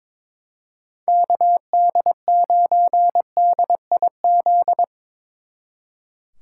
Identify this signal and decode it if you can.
Morse code, heard as a keyed tone, about 24 wpm, 700 Hz. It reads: KD9DIZ
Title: Speed Racing - USA 2x3 Call Signs 22wpm